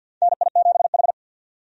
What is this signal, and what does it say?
Morse code, heard as a keyed tone, about 53 wpm, 700 Hz. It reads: NI7H
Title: USA Call Signs - 4 Characters - 50wpm